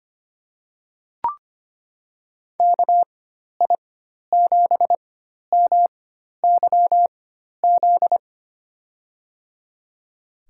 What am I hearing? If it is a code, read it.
Morse code, heard as a keyed tone, about 25 wpm, spaced as for 12 wpm, 700 Hz. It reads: KI7MYZ
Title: USA Call Signs - 6 Characters - 4x Character Spacing - 25wpm